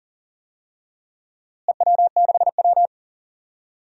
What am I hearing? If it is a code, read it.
Morse code, heard as a keyed tone, about 37 wpm, 700 Hz. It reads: EW6W
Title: DXCC-100 Call Signs - 4 Characters - Repeated 2x - 40wpm